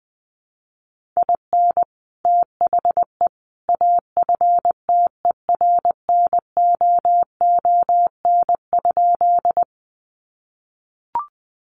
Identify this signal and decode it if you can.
Morse code, heard as a keyed tone, about 20 wpm, 700 Hz. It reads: IN THE AFTERNOON?
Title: Sentences from Top 1500 Words 20wpm